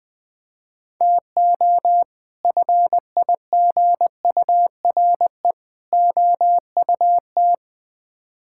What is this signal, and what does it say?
Morse code, heard as a keyed tone, about 20 wpm, 700 Hz. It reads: TO FIGURE OUT
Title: Build A Phrase - 5-Word Phrases - 20wpm